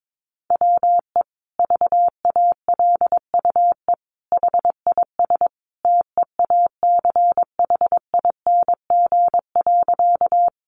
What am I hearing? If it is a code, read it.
Morse code, heard as a keyed tone, about 22 wpm, 700 Hz. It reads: WE VALUE HIS TEACHING.
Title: Sentences from Top 1100 Words 22wpm